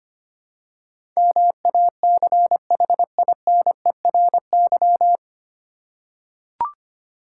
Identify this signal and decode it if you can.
Morse code, heard as a keyed tone, about 25 wpm, 700 Hz. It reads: MACHINERY
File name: Nine Letter Words - 25wpm